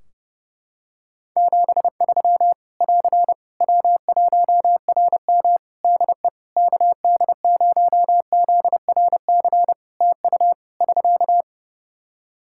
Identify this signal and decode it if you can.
Morse code, heard as a keyed tone, about 32 wpm, 700 Hz. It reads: 73 <AR> W1RM DE KD0ZRC TU <SK>
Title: Practice QSO - Exchange 4 - Ending - 30wpm